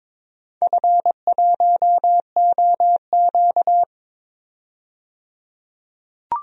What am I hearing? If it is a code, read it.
Morse code, heard as a keyed tone, about 22 wpm, 700 Hz. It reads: F1OQ
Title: DXCC-100 Call Signs - 4 Characters - 22wpm